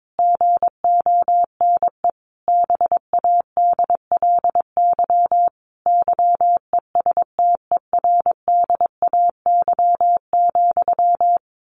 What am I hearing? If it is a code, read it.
Morse code, heard as a keyed tone, about 22 wpm, 700 Hz. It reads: GONE BADLY YESTERDAY,
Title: A Sound Of Thunder 22wpm